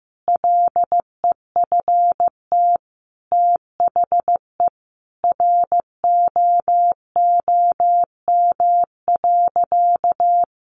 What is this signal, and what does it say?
Morse code, heard as a keyed tone, about 15 wpm, 700 Hz. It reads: LEFT THE ROOM.
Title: Sentences from Top 300 Words 15wpm